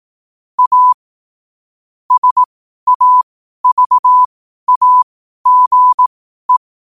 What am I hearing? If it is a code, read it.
Morse code, heard as a keyed tone, about 18 wpm, spaced as for 12 wpm, 1000 Hz. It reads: A SAVAGE
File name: qotdfiles / 12wpm